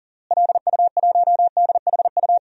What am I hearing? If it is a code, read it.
Morse code, heard as a keyed tone, about 40 wpm, 700 Hz. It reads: LU1DHU
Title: DXCC-100 Call Signs - 6 Characters - Repeated 2x - 40wpm